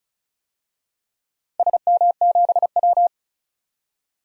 Morse code, heard as a keyed tone, about 36 wpm, 700 Hz. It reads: SM7W
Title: DXCC-100 Call Signs - 4 Characters - 35wpm